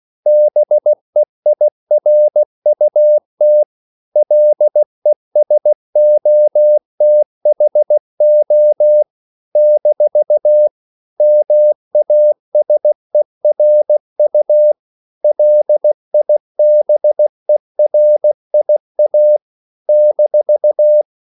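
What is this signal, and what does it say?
Morse code, heard as a keyed tone, about 16 wpm, 600 Hz. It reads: BEIRUT LESOTHO - MASERU LIBERIA -
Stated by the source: CW practice of national capitals at 16 words per minute, from ON6ZQ.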